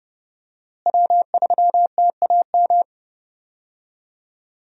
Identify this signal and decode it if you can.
Morse code, heard as a keyed tone, about 32 wpm, 700 Hz. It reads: W3TAM
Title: USA Call Signs - 5 Characters - Repeated 3x - 30wpm